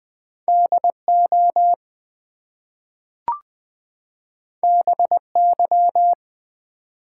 Morse code, heard as a keyed tone, about 20 wpm, 700 Hz. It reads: DO BY
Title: Two Letter Words - 20wpm